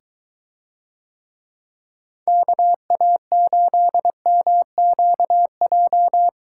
Morse code, heard as a keyed tone, about 23 wpm, 700 Hz. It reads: KA8MQJ